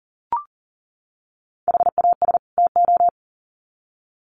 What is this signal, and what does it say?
Morse code, heard as a keyed tone, about 38 wpm, 700 Hz. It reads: HAS TO
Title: Sets of 2 Words 40wpm